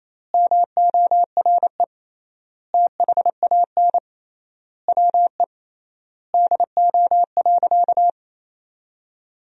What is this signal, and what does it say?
Morse code, heard as a keyed tone, about 28 wpm, 700 Hz. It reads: MORE THAN WE DO.
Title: 3x Word-Spacing - Sentences from Top 100 Words - 28wpm